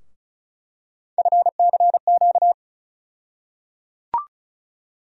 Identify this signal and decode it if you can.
Morse code, heard as a keyed tone, about 36 wpm, 700 Hz. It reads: FCQ
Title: USA Call Sign Postfixes - 3 Letters 35wpm